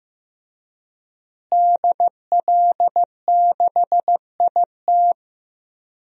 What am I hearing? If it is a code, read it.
Morse code, heard as a keyed tone, about 15 wpm, 700 Hz. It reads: DL6IT
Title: DXCC-100 Call Signs - 5 Characters - Repeated 2x - 15wpm